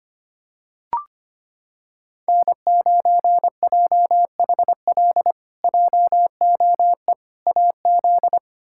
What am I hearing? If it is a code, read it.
Morse code, heard as a keyed tone, about 25 wpm, 700 Hz. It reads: N9JHL JOE AZ
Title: K1USN SST QSO Practice - 25wpm